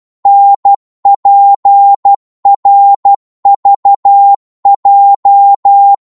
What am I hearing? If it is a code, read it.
Morse code, heard as a keyed tone, about 12 wpm, 800 Hz. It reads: NPRVJ
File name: Morse / ltr-12wpm